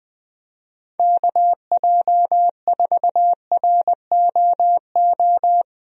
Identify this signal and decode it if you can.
Morse code, heard as a keyed tone, about 20 wpm, 700 Hz. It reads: KJ4ROO
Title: USA Call Signs - 6 Characters - 20wpm